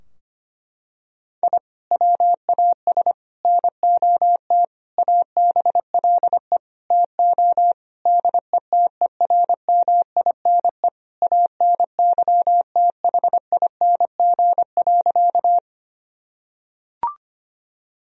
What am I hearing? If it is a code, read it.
Morse code, heard as a keyed tone, about 25 wpm, 700 Hz. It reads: I WAS NOT ABLE TO DETERMINE ANYTHING.
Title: Sentences from Top 1000 Words - Review Part 4 - 25wpm